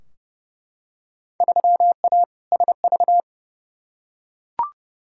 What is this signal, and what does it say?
Morse code, heard as a keyed tone, about 32 wpm, 700 Hz. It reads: 3A SV
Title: Field Day Exchanges - 30wpm